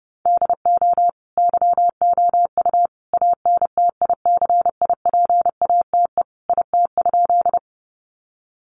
Morse code, heard as a keyed tone, about 28 wpm, 700 Hz. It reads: DO YOU ANTICIPATE IT?
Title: Sentences from Top 1500 Words 30wpm